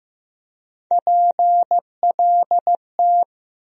Morse code, heard as a keyed tone, about 15 wpm, 700 Hz. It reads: PLT
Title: USA Call Sign Postfixes - 3 Letters 15wpm